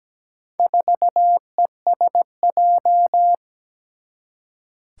Morse code, heard as a keyed tone, about 17 wpm, 700 Hz. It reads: 4ESJ
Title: Four Challenging Letter-Number Combinations 17wpm